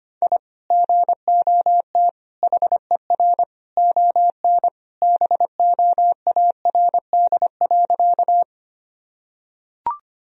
Morse code, heard as a keyed tone, about 25 wpm, 700 Hz. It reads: I GOT HER ON BOARD.
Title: Sentences from Top 1000 Words - Review Part 4 - 25wpm